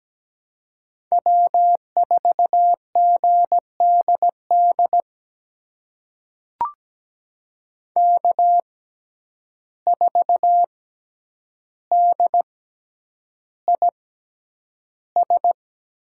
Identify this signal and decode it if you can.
Morse code, heard as a keyed tone, about 17 wpm, spaced as for 9 wpm, 700 Hz. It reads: W4GDD K4DIS
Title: USA Call Signs - 5 Characters - 6x Character Spacing - 17wpm